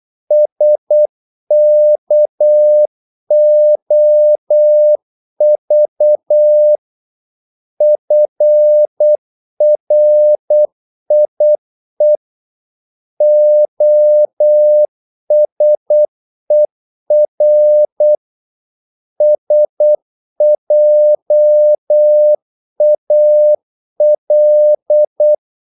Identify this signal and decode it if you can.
Morse code, heard as a keyed tone, about 8 wpm, 600 Hz. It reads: SKOV FRIE OSER SJAL
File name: Korte ord DK 8wpm